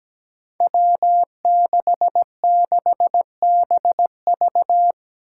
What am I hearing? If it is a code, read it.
Morse code, heard as a keyed tone, about 17 wpm, 700 Hz. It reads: W66BV
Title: Five Challenging Letter-Number Combinations 17wpm